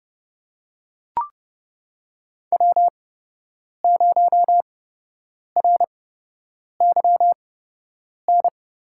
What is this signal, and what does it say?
Morse code, heard as a keyed tone, about 31 wpm, spaced as for 8 wpm, 700 Hz. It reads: W0RYN
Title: USA Call Signs - 5 Characters - 8x Character Spacing - 30wpm